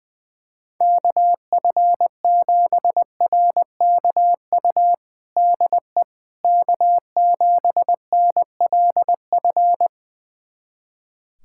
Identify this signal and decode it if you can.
Morse code, heard as a keyed tone, about 20 wpm, 700 Hz. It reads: KF7RKU DE K7NLF
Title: USA Callsign DE Callsign 20wpm